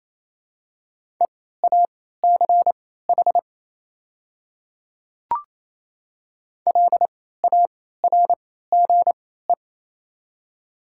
Morse code, heard as a keyed tone, about 28 wpm, spaced as for 16 wpm, 700 Hz. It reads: EACH LARGE
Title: 3x Character Spacing - Top 200 Words - 28wpm